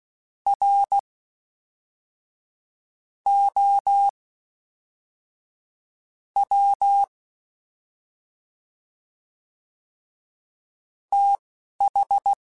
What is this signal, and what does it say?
Morse code, heard as a keyed tone, about 16 wpm, spaced as for 5 wpm, 800 Hz. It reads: ROW TH